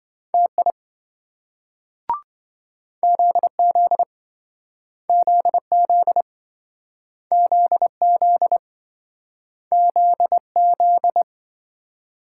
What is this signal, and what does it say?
Morse code, heard as a keyed tone, about 26 wpm, 700 Hz. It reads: TI ZZ ZZ ZZ ZZ